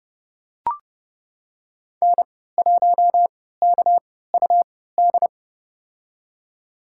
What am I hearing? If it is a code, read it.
Morse code, heard as a keyed tone, about 32 wpm, spaced as for 17 wpm, 700 Hz. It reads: N1KUD
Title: USA Call Signs - 5 Characters - 3x Character Spacing - 30wpm